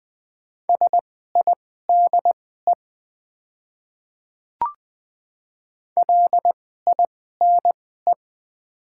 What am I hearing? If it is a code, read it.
Morse code, heard as a keyed tone, about 20 wpm, spaced as for 14 wpm, 700 Hz. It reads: SIDE LINE
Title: 2x Character Spacing - Top 200 Words - 20wpm